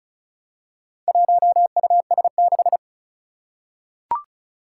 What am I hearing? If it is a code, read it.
Morse code, heard as a keyed tone, about 36 wpm, 700 Hz. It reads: 1US6
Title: Four Challenging Letter-Number Combinations 35wpm